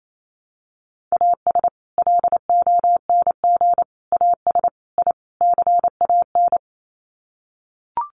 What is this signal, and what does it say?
Morse code, heard as a keyed tone, about 28 wpm, 700 Hz. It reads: AS LONG AS I CAN
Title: Sets of 5 Words Encore2 28wpm